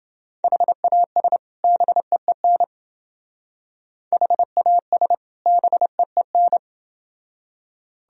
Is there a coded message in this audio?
Yes: Morse code from 0.4 to 6.6 s, heard as a keyed tone, about 28 wpm, 700 Hz. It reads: HAS BEEN HAS BEEN